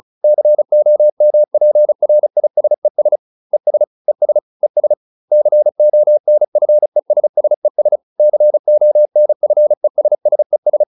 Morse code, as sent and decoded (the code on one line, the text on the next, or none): -.-. --- -- .--. .-. .. ... ... / . ... / . ... / . ... / -.-. --- -. ..-. ... ... ... / -.-. --- -. ..-. ... ... ...
COMPRISES ES ES ES CONFESSES CONFESSES